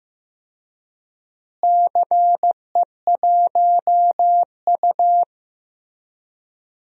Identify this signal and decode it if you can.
Morse code, heard as a keyed tone, about 15 wpm, 700 Hz. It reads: CE1U
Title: DXCC-100 Call Signs - 4 Characters - Repeated 3x - 15wpm